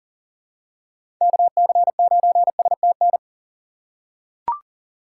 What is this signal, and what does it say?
Morse code, heard as a keyed tone, about 40 wpm, 700 Hz. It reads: KC9STN
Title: USA Call Signs - 6 Characters - 2x Character Spacing - 40wpm